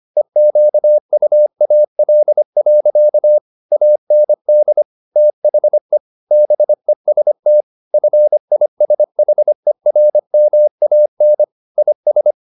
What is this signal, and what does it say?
Morse code, heard as a keyed tone, about 25 wpm, 600 Hz. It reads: EQUAL. AND THE BEST FISHERMAN IS
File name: the old man and the sea part 1 25wpm